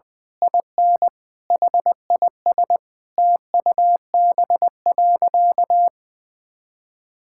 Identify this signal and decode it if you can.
Morse code, heard as a keyed tone, about 21 wpm, 700 Hz. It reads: IN HIS TUB.
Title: Pangram Challenge v2 - 20wpm